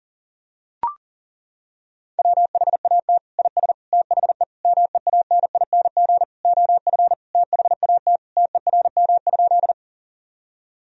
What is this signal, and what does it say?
Morse code, heard as a keyed tone, about 40 wpm, 700 Hz. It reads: WHAT IS THE MEANING OF THAT TERM?
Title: Sentences from Top 1000 Words - Review Part 4 - 40wpm